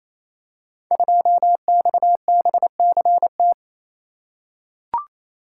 Morse code, heard as a keyed tone, about 28 wpm, 700 Hz. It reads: 2XBCT
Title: Code Groups - 5 Letter:Numbers - 28wpm